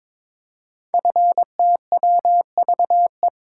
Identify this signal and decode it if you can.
Morse code, heard as a keyed tone, about 22 wpm, 700 Hz. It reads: FTWVE
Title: Code Groups - 5 Letters - 22wpm